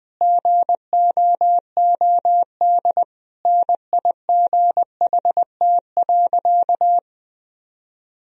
Morse code, heard as a keyed tone, about 20 wpm, 700 Hz. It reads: GOOD NIGHT.
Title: Sentences from Top 1000 Words - Review Part 4 - 20wpm